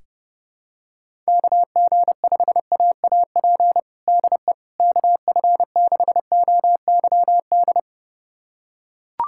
Morse code, heard as a keyed tone, about 32 wpm, 700 Hz. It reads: KG5AAP DE KF6OYD
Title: USA Callsign DE Callsign - 6 Characters - 30wpm